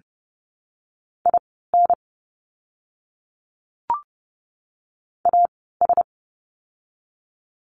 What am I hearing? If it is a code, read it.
Morse code, heard as a keyed tone, about 30 wpm, spaced as for 17 wpm, 700 Hz. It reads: IN AS
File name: course / 30:10wpm / 115 Lesson 19f - Letter w - Words 30wpm